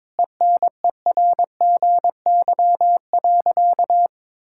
Morse code, heard as a keyed tone, about 22 wpm, 700 Hz. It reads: ENERGY.
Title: Sentences from Top 1000 Words - Review Part 6 - 22wpm